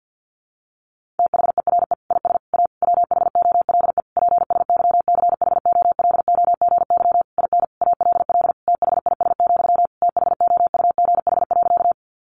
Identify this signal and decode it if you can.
Morse code, heard as a keyed tone, about 50 wpm, 700 Hz. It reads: THERE IS A WHOLE PSYCHOLOGY IN ALL THIS, THOUGH.
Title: Sentences from Top 1400 Words 50wpm